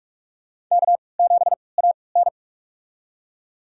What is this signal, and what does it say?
Morse code, heard as a keyed tone, about 42 wpm, spaced as for 25 wpm, 700 Hz. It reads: K7AN
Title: USA Call Signs - 4 Characters - 3x Character Spacing - 45wpm